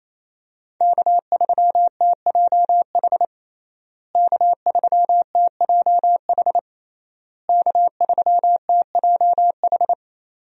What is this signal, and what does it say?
Morse code, heard as a keyed tone, about 28 wpm, 700 Hz. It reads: K3TJH K3TJH K3TJH
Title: USA Call Signs - 5 Characters - Repeated 3x - 28wpm